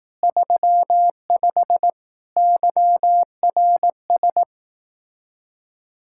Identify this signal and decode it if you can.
Morse code, heard as a keyed tone, about 18 wpm, 700 Hz. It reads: 35 YRS
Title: Speed-Racing Common QSO Phrases Encore - 15wpm